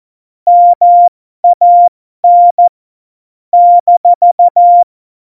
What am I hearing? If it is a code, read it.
Morse code, heard as a keyed tone, about 14 wpm, spaced as for 12 wpm, 700 Hz. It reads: MAN -
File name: Morsekurs / Witze